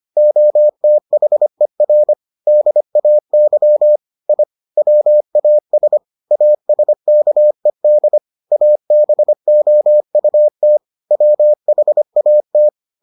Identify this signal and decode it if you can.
Morse code, heard as a keyed tone, about 25 wpm, 600 Hz. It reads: OTHER DAY I WAS ASKED ABOUT WHAT